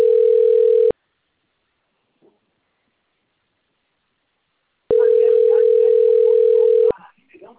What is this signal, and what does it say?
Ringback tone (North America), 440 + 480 Hz.